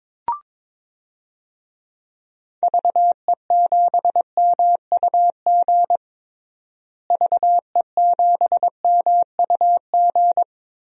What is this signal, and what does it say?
Morse code, heard as a keyed tone, about 22 wpm, 700 Hz. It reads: VE7MUG VE7MUG
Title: DXCC-100 Call Signs - 6 Characters - Repeated 3x - 22wpm